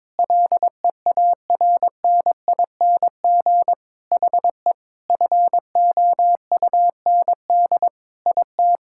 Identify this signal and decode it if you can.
Morse code, heard as a keyed tone, about 22 wpm, 700 Hz. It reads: LEARNING HE FOUND IT
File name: practice / 22wpm / Sentences with 4-Word Phrases - 22wpm